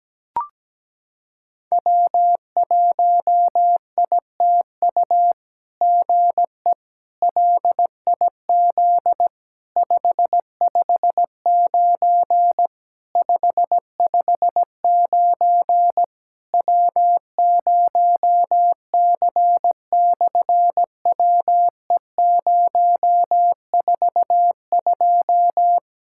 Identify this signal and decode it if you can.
Morse code, heard as a keyed tone, about 17 wpm, 700 Hz. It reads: W1ITU GE LIZ 559 559 W0C/WE042
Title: SOTA QSO Practice 17wpm